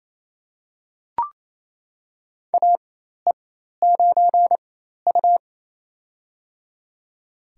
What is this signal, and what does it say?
Morse code, heard as a keyed tone, about 28 wpm, spaced as for 13 wpm, 700 Hz. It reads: AE9U
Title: USA Call Signs - 4x Character Spacing - 28wpm